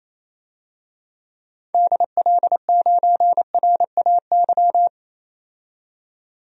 Morse code, heard as a keyed tone, about 27 wpm, 700 Hz. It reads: DL9RAY
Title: DXCC-100 Call Signs - 6 Characters - 28wpm